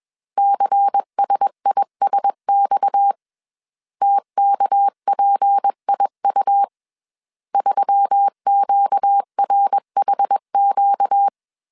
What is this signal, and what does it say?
Morse code, heard as a keyed tone, about 21 wpm, 800 Hz. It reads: CSISX TKPIU 3QRHQ
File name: CW / DET / 1 6 1500C S B 20wpm 2048